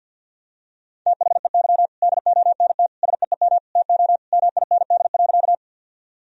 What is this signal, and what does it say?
Morse code, heard as a keyed tone, about 48 wpm, 700 Hz. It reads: THEY DONT SEEM TO MIND.
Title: Sentences from Top 1000 Words - Review Part 1 - 50wpm